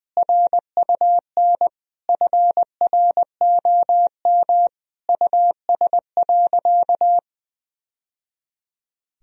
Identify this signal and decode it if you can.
Morse code, heard as a keyed tone, about 21 wpm, 700 Hz. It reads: RUN FROM US.
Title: Sentences from Top 1000 Words - Review Part 1 - 20wpm